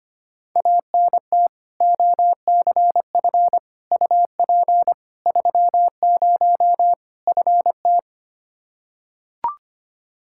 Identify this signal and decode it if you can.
Morse code, heard as a keyed tone, about 25 wpm, 700 Hz. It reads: ANT OCF UP 30 FT
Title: Practice QSO - Exchange 2 - Info - 25wpm